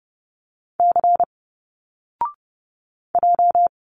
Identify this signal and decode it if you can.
Morse code, heard as a keyed tone, about 30 wpm, 700 Hz. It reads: C J
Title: Single Letters 30wpm